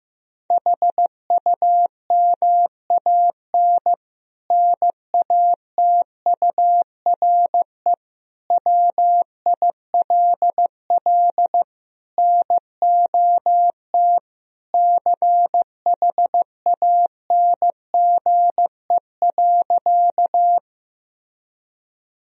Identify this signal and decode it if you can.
Morse code, heard as a keyed tone, about 15 wpm, 700 Hz. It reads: HUMAN NATURE WILL NOT CHANGE.
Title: Sentences from Top 1000 Words - Review Part 3 - 15wpm